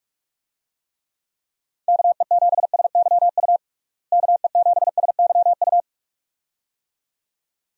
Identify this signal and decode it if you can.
Morse code, heard as a keyed tone, about 46 wpm, 700 Hz. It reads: KE7SYU KE7SYU
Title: USA Call Signs - 6 Characters - Repeated 2x - 45wpm